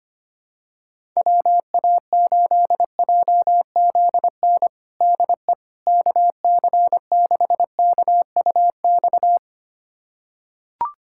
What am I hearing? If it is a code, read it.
Morse code, heard as a keyed tone, about 25 wpm, 700 Hz. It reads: WA8JZN DE KC6KUX
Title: USA Callsign DE Callsign - 6 Characters - 25wpm